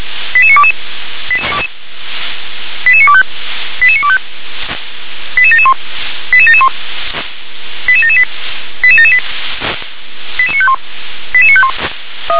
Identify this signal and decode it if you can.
Five-tone selective call (ZVEI), heard as quick runs of five tones, tones of about 70 ms. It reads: 88922 88925 88925 88771 88771 88778 88778 88951 88951
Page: SDR/ZVEI